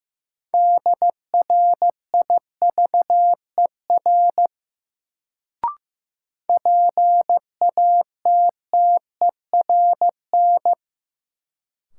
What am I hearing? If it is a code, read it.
Morse code, heard as a keyed tone, about 15 wpm, 700 Hz. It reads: DRIVER PATTERN
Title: Sentences from Top 1000 Words - New Word Drill - 15wpm